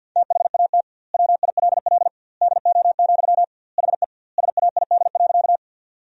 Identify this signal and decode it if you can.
Morse code, heard as a keyed tone, about 50 wpm, 700 Hz. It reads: THAT WILL DO, HE SAID.
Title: 2x Word-Spacing - Sentences from Top 100 Words - 50wpm